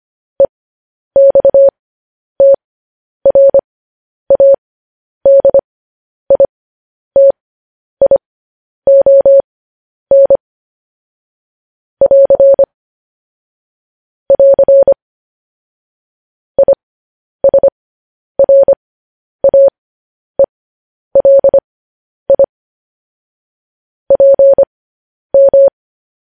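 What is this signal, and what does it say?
Morse code, heard as a keyed tone, about 25 wpm, spaced as for 10 wpm, 550 Hz. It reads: EXTRADITION <AR> <AR> ISRAELI PM
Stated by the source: Morse code news headlines at 10 wpm.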